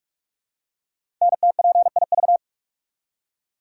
Morse code, heard as a keyed tone, about 44 wpm, 700 Hz. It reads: NTPIV
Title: Code Groups - 5 Letter:Numbers - 45wpm